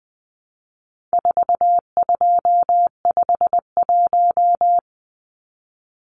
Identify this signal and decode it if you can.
Morse code, heard as a keyed tone, about 20 wpm, 700 Hz. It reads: 4251
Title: Numbers - 4-Digit 20wpm